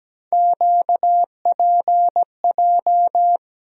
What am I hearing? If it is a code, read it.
Morse code, heard as a keyed tone, about 17 wpm, 700 Hz. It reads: QPJ
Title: USA Call Sign Postfixes - 3 Letters 17wpm